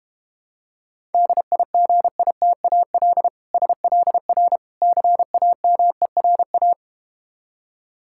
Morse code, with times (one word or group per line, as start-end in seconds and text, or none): 1.1-3.3 DIGITAL
3.5-4.6 SLR
4.8-6.7 CAMERA